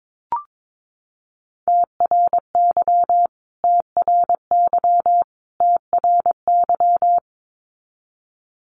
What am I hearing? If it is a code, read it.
Morse code, heard as a keyed tone, about 22 wpm, 700 Hz. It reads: TRY TRY TRY
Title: Top 500 Words Repeated 22wpm